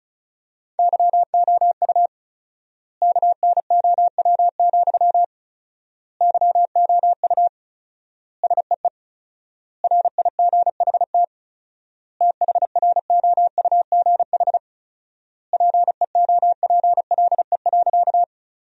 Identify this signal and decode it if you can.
Morse code, heard as a keyed tone, about 36 wpm, 700 Hz. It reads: YOU KNOW, YOU SEE RIGHT THROUGH PEOPLE.
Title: 4x Word-Spacing - Sentences from Top 200 Words - 35wpm